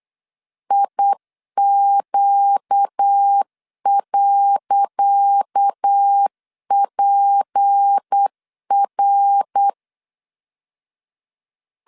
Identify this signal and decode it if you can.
Morse code, heard as a keyed tone, about 8 wpm, 800 Hz. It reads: IQ.PR
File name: LEZIONI / 10 SIGNS / 10C DOT 8wpm 3649